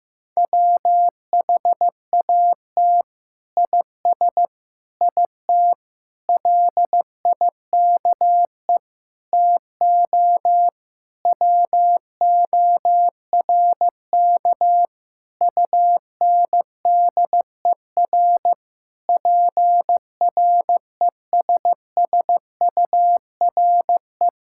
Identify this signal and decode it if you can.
Morse code, heard as a keyed tone, about 15 wpm, 700 Hz. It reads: WHAT IS IT LIKE TO WORK UNDER PRESSURE
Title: Sentences from Top 1000 Words - Review Part 6 - 15wpm